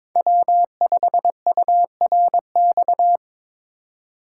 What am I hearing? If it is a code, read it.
Morse code, heard as a keyed tone, about 22 wpm, 700 Hz. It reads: W5URX
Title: USA Call Signs - 5 Characters - 2x Character Spacing - 22wpm